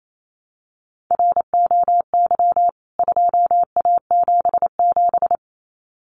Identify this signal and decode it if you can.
Morse code, heard as a keyed tone, about 28 wpm, 700 Hz. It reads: ROY 2A77
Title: CWT Exchange 28wpm